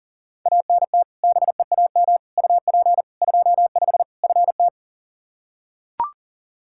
Morse code, heard as a keyed tone, about 40 wpm, 700 Hz. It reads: ANT BEAM UP 25 FT
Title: K1USN SST QSO Practice 40wpm